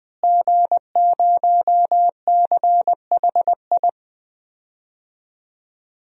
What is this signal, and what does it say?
Morse code, heard as a keyed tone, about 21 wpm, 700 Hz. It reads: G0CHI
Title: DXCC-100 Call Signs - 5 Characters - Repeated 2x - 20wpm